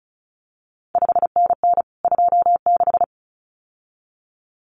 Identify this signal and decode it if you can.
Morse code, heard as a keyed tone, about 36 wpm, 700 Hz. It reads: 5NN 26
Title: CQ WW DX Contest Exchange Encore - 35wpm